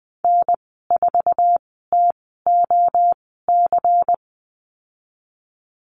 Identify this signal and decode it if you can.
Morse code, heard as a keyed tone, about 20 wpm, spaced as for 15 wpm, 700 Hz. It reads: N4TOC
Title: 173 Lesson 28g - Number 9 - Call Signs 20wpm